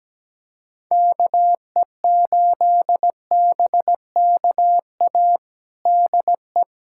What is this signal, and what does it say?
Morse code, heard as a keyed tone, about 17 wpm, 700 Hz. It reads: KE8BKA DE
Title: USA Callsign DE Callsign - 6 Characters - 17wpm